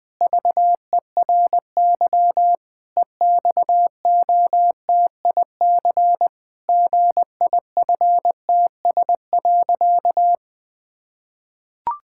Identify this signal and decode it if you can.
Morse code, heard as a keyed tone, about 20 wpm, 700 Hz. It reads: VERY EXOTIC GIFTS.
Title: Pangram Challenge v2 - 20wpm